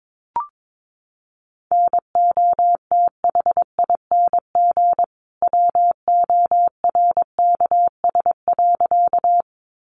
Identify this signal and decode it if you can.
Morse code, heard as a keyed tone, about 22 wpm, 700 Hz. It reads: NOTHING WORKS.
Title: Additional Easy Phrases 22wpm